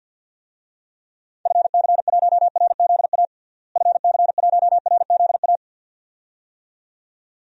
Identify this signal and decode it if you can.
Morse code, heard as a keyed tone, about 48 wpm, 700 Hz. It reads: FC1RZA FC1RZA
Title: DXCC-100 Call Signs - 6 Characters - Repeated 2x - 50wpm